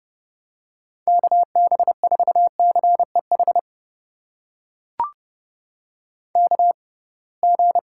Morse code, heard as a keyed tone, about 30 wpm, spaced as for 23 wpm, 700 Hz. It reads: KB4CEH KG
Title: USA Call Signs - 6 Characters - 6x Character Spacing - 30wpm